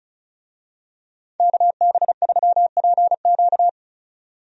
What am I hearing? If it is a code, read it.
Morse code, heard as a keyed tone, about 34 wpm, 700 Hz. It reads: KB3PQ
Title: USA Call Signs - 5 Characters - 35wpm